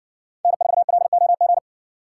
Morse code, heard as a keyed tone, about 67 wpm, 700 Hz. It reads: N4LOG